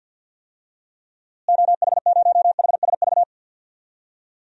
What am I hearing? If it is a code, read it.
Morse code, heard as a keyed tone, about 52 wpm, 700 Hz. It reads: OH0HSV
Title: DXCC-100 Call Signs - 6 Characters - Repeated 2x - 50wpm